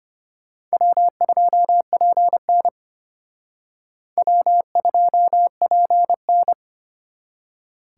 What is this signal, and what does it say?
Morse code, heard as a keyed tone, about 32 wpm, 700 Hz. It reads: W2PN W2PN